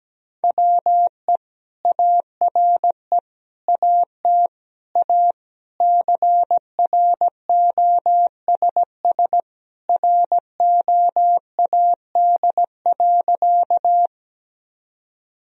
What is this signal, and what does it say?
Morse code, heard as a keyed tone, about 17 wpm, 700 Hz. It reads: WE ARE AT A CROSS ROAD.
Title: Sentences from Top 1000 Words - Review Part 2 - 17wpm